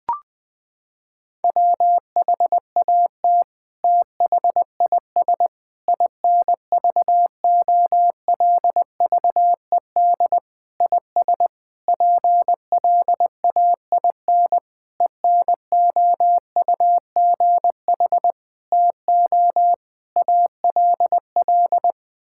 Morse code, heard as a keyed tone, about 21 wpm, 700 Hz. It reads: WHAT THIS INVOLVED IS PLAIN ENOUGH TO ALL
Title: Sentences from Top 1000 Words - Review Part 5 - 20wpm